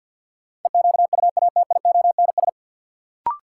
Morse code, heard as a keyed tone, about 51 wpm, 700 Hz. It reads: EQUATIONS
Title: Sets of 5 Words - New Word Drill - 50wpm